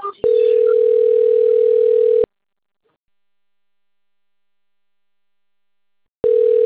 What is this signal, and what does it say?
Ringback tone (North America), 440 + 480 Hz.
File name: out- - - - - .64147